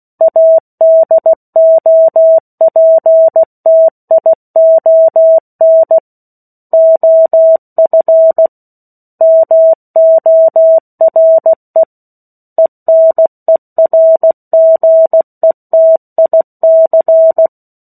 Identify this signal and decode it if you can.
Morse code, heard as a keyed tone, about 16 wpm, 650 Hz. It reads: ADOPTION OF MORE ENERGETIC
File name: 03-Chapter-03 16 WPM